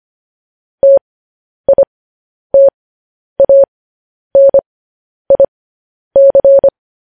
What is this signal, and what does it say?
Morse code, heard as a keyed tone, about 25 wpm, spaced as for 10 wpm, 550 Hz. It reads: TITANIC
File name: podcast / News headlines-1x-25x10